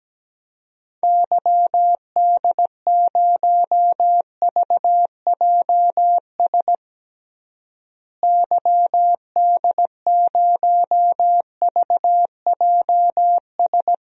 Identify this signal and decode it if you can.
Morse code, heard as a keyed tone, about 17 wpm, 700 Hz. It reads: YD0VJS YD0VJS
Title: DXCC-100 Call Signs - 6 Characters - Repeated 2x - 17wpm